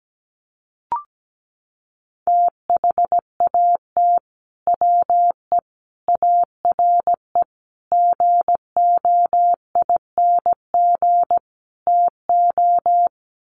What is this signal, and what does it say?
Morse code, heard as a keyed tone, about 17 wpm, 700 Hz. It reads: THAT WE ARE GOING TO
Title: Sets of 5 Words Encore 17wpm